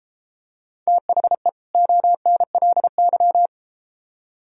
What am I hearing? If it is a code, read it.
Morse code, heard as a keyed tone, about 33 wpm, 700 Hz. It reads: THE ONLY